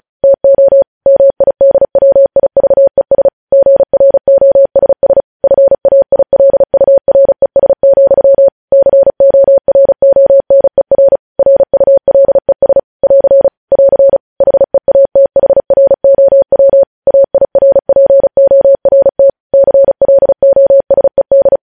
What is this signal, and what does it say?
Morse code, heard as a keyed tone, about 36 wpm, 550 Hz. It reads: TO MIDWIVES GROSS FAILURES, CORONER RULES <AR> <AR> HEATHROW AIRPORT CLOSED
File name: podcast / News headlines-1x-35x35